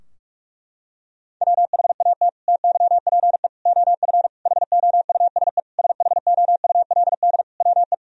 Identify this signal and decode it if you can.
Morse code, heard as a keyed tone, about 47 wpm, 700 Hz. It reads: WHAT TYPE OF HOUSE SHOULD WE
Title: Sentences from Top 1000 Words - Review Part 3 - 45wpm